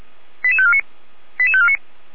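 Five-tone selective call (ZVEI), heard as quick runs of five tones, tones of about 70 ms. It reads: 88549 88549
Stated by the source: Five-tone ZVEI selective call received on SDR, logged by the Feuerwehr Orlamünde fire brigade (f85360) audio server.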